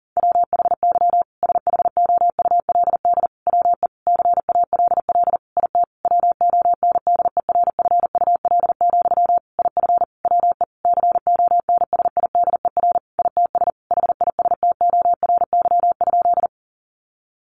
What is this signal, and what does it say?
Morse code, heard as a keyed tone, about 40 wpm, 700 Hz. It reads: WHY SHOULD WE CALL IT WONDERFUL, IF WE CONSIDER ITS HISTORY?
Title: Sentences from Top 1100 Words 40wpm